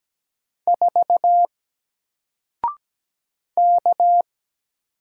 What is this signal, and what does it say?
Morse code, heard as a keyed tone, about 17 wpm, 700 Hz. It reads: 4 K
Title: Single Letter-Number Alt 17wpm